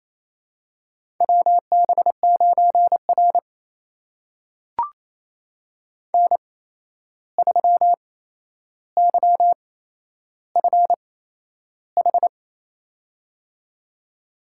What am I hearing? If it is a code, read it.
Morse code, heard as a keyed tone, about 28 wpm, spaced as for 11 wpm, 700 Hz. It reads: WB9R N3YFH
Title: USA Call Signs - 8x Character Spacing - 28wpm